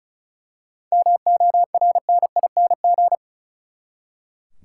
Morse code, heard as a keyed tone, about 36 wpm, 700 Hz. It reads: MORNING
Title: Seven Letter Words - 35wpm